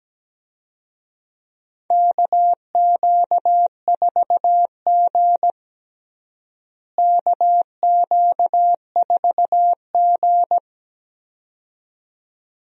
Morse code, heard as a keyed tone, about 17 wpm, 700 Hz. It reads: KQ4G KQ4G
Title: USA Call Signs - 4 Characters - Repeated 2x - 17wpm